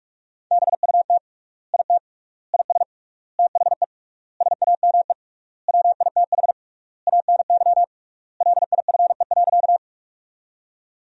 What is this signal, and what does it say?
Morse code, heard as a keyed tone, about 41 wpm, 700 Hz. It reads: BUT IT IS THE SAME WITH ANY LIFE.
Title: 3x Word-Spacing - Sentences from Top 300 Words - 45wpm